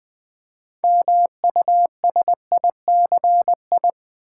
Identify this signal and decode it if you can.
Morse code, heard as a keyed tone, about 20 wpm, 700 Hz. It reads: MUSICI
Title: Build A Word - 3 Letters At A Time - 8-Letter Words - 20wpm